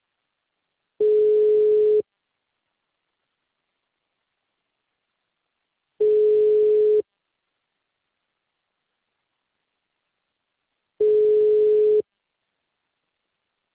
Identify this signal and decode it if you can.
Ringback tone (Europe, 425 Hz).